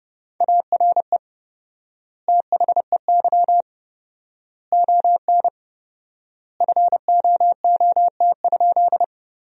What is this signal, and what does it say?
Morse code, heard as a keyed tone, about 29 wpm, 700 Hz. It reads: ARE THEY ON FOOT?
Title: 4x Word-Spacing - Sentences from Top 500 Words - 30wpm